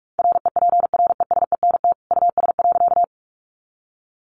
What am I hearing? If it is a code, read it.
Morse code, heard as a keyed tone, about 44 wpm, 700 Hz. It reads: REPRESENT US.
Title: Sentences from Top 1200 Words 45wpm